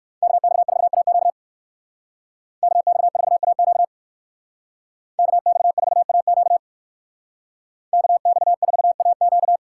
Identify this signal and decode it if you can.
Morse code, heard as a keyed tone, about 59 wpm, 700 Hz. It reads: KX4AQ KX4AQ KX4AQ KX4AQ